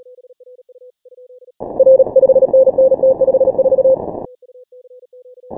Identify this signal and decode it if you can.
Morse code, heard as a keyed tone, about 33 wpm, 530 Hz. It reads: R 5NN T54
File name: final / W1UJ-168-140-MorseRunner